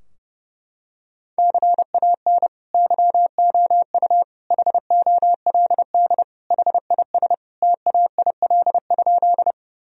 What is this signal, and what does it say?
Morse code, heard as a keyed tone, about 30 wpm, 700 Hz. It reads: CAN YOU HOLD HIS TAIL?
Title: Sentences from Top 1000 Words - Review Part 3 - 30wpm